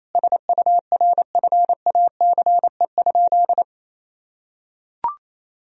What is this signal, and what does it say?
Morse code, heard as a keyed tone, about 28 wpm, 700 Hz. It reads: SURFACE?
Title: Sentences from Top 1000 Words - Review Part 3 - 28wpm